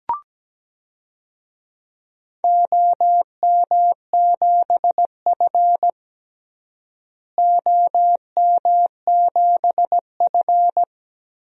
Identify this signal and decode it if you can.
Morse code, heard as a keyed tone, about 17 wpm, 700 Hz. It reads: OM7F OM7F
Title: DXCC-100 Call Signs - 4 Characters - Repeated 2x - 17wpm